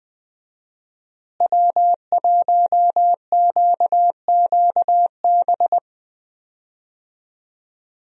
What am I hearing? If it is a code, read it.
Morse code, heard as a keyed tone, about 20 wpm, 700 Hz. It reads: W1QQB
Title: USA Call Signs - 5 Characters - Repeated 2x - 20wpm